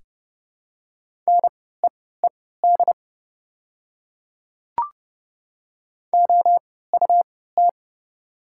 Morse code, heard as a keyed tone, about 30 wpm, spaced as for 17 wpm, 700 Hz. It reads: NEED OUT
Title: 3x Character Spacing - Top 300 Words - 30wpm